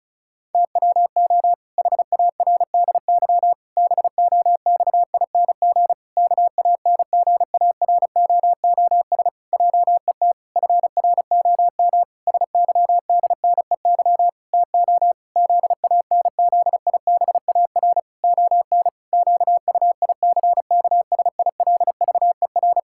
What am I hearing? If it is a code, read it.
Morse code, heard as a keyed tone, about 34 wpm, 700 Hz. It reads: TWO HARDY BOXING KANGAROOS JET FROM SYDNEY TO ZANZIBAR ON QUICKSILVER
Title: Pangram Challenge v2 - 35wpm